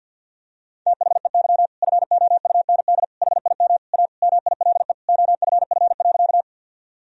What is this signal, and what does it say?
Morse code, heard as a keyed tone, about 52 wpm, 700 Hz. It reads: THEY FOUND HIM A MILE OFF.
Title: Sentences from Top 1000 Words - Review Part 1 - 50wpm